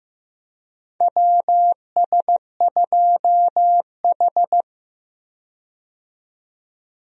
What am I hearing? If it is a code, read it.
Morse code, heard as a keyed tone, about 15 wpm, 700 Hz. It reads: WS2H
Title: USA Call Signs - 4 Characters - 15wpm